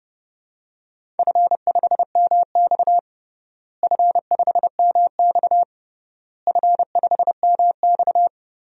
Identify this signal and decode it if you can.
Morse code, heard as a keyed tone, about 32 wpm, 700 Hz. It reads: F5MX F5MX F5MX
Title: DXCC-100 Call Signs - 4 Characters - Repeated 3x - 30wpm